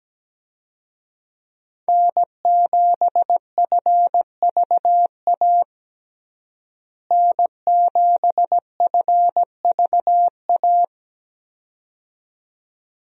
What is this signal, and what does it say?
Morse code, heard as a keyed tone, about 17 wpm, 700 Hz. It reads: N7FVA N7FVA
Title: USA Call Signs - 5 Characters - Repeated 2x - 17wpm